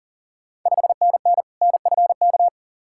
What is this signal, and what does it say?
Morse code, heard as a keyed tone, about 40 wpm, 690 Hz. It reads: 5NN NFK
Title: Ontario QSO Party Exchange 40wpm